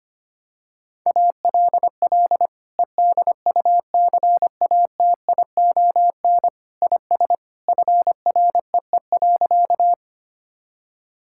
Morse code, heard as a keyed tone, about 25 wpm, 700 Hz. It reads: ALL EDUCATION IS FREE.
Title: Sentences from Top 1000 Words - Review Part 6 - 25wpm